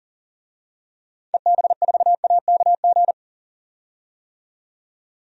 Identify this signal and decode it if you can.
Morse code, heard as a keyed tone, about 40 wpm, 700 Hz. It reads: EB4AKG
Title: DXCC-100 Call Signs - 6 Characters - 40wpm